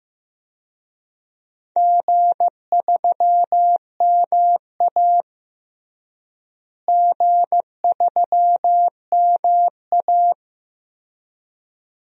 Morse code, heard as a keyed tone, about 15 wpm, 700 Hz. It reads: G3MA G3MA
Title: DXCC-100 Call Signs - 4 Characters - Repeated 2x - 15wpm